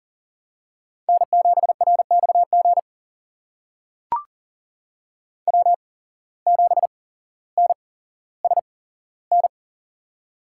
Morse code, heard as a keyed tone, about 40 wpm, spaced as for 17 wpm, 700 Hz. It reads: N7RXG W7NSN
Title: USA Call Signs - 5 Characters - 8x Character Spacing - 40wpm